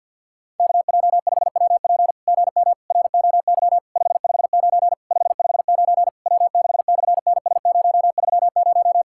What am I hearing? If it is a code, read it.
Morse code, heard as a keyed tone, about 50 wpm, 700 Hz. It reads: KJ5WP GM ROY 559 559 W6/NS030
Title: SOTA QSO Practice 50wpm